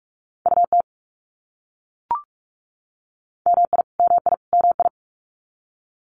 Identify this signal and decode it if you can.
Morse code, heard as a keyed tone, about 44 wpm, 700 Hz. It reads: UT MI MI MI
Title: US States Repeated 3x 45wpm